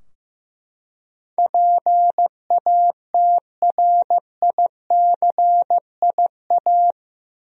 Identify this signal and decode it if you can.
Morse code, heard as a keyed tone, about 15 wpm, 700 Hz. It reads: PATRICIA
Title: USA Common Names - 15wpm